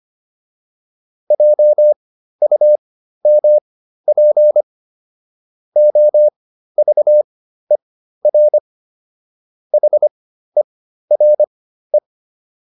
Morse code, heard as a keyed tone, about 25 wpm, spaced as for 13 wpm, 600 Hz. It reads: JUMP OVER HERE
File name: PR405 13